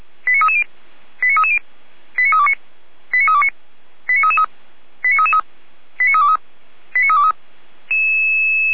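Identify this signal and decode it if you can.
Five-tone selective call (ZVEI), heard as quick runs of five tones, tones of about 70 ms. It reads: 89330 89330 89329 89329 89333 89333 89323 89323
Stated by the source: ZVEI five-tone selective call 89510, received by SDR.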